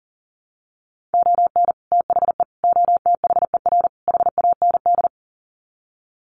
Morse code, heard as a keyed tone, about 40 wpm, 700 Hz. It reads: ON THE OTHER HAND
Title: Sets of 4 Words 40wpm